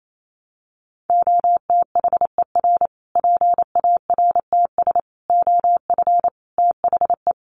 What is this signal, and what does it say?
Morse code, heard as a keyed tone, about 28 wpm, 700 Hz. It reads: OTHER PARTS OF THE
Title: Sets of 5 Words Encore 28wpm